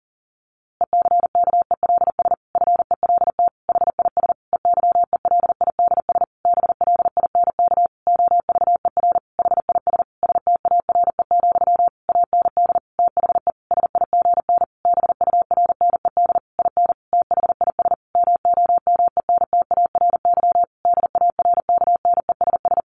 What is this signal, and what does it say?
Morse code, heard as a keyed tone, about 40 wpm, 700 Hz. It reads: ECKELS FELT HIS EYELIDS BLINK OVER HIS STARE, AND THE SIGN BURNED IN THIS MOMENTARY DARKNESS
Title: A Sound Of Thunder 40wpm